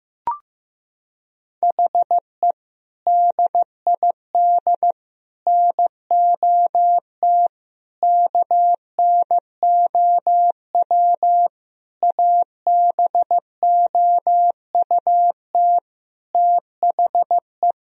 Morse code, heard as a keyed tone, about 15 wpm, 700 Hz. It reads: HE DID NOT KNOW ABOUT THE
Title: Sentences from Top 1000 Words - Review Part 3 - 15wpm